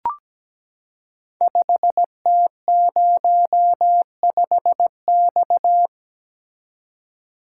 Morse code, heard as a keyed tone, about 17 wpm, 700 Hz. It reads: 5T05X
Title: Code Groups - 5 Letter:Numbers - 17wpm